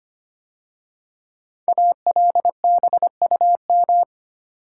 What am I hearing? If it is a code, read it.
Morse code, heard as a keyed tone, about 25 wpm, 700 Hz. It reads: ALBUM
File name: 5 Letter Words - First 3 Letters Given - 25wpm